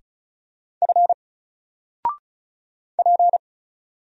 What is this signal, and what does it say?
Morse code, heard as a keyed tone, about 33 wpm, 700 Hz. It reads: F P
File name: Single-Letter-Number-35wpm